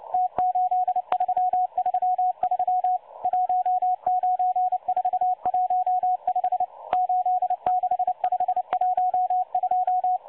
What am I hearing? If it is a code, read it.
Morse code, heard as a keyed tone, about 29 wpm, 730 Hz. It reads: T8333 19415 86512
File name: M12-Morse-Numbers-Station-8116-kHz-CW